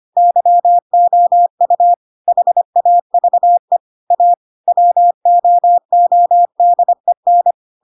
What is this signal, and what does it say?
Morse code, heard as a keyed tone, about 25 wpm, 700 Hz. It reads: YOU HAVE A WOODEN
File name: Pinocchio / Pinocchio ch4 25 25 ik1hge